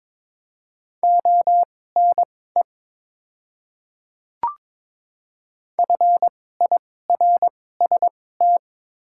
Morse code, heard as a keyed tone, about 22 wpm, spaced as for 16 wpm, 700 Hz. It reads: ONE FIRST
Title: 2x Character Spacing - Top 100 Words - 22wpm